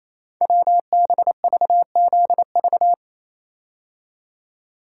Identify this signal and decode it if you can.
Morse code, heard as a keyed tone, about 28 wpm, 700 Hz. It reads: WBVZV
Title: Code Groups - 5 Letters - 28wpm